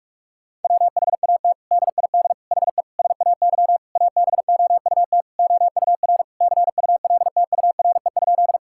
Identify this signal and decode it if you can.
Morse code, heard as a keyed tone, about 46 wpm, 700 Hz. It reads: WHAT DID HE SAY ABOUT OUR CULTURE?
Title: Sentences from Top 1000 Words - Review Part 6 - 45wpm